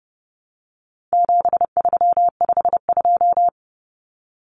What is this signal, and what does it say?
Morse code, heard as a keyed tone, about 30 wpm, 700 Hz. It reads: 7352
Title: Numbers - 4-Digit 30wpm